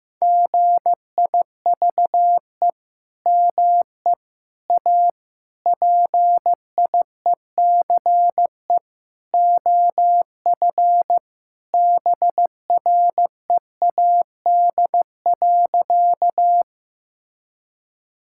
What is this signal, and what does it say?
Morse code, heard as a keyed tone, about 16 wpm, 700 Hz. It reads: GIVE ME A PIECE OF BREAD.
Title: Sentences from Top 1600 Words 15wpm